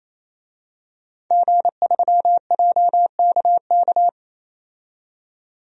Morse code, heard as a keyed tone, about 28 wpm, 700 Hz. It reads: G3JKK
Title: DXCC-100 Call Signs - 5 Characters - 28wpm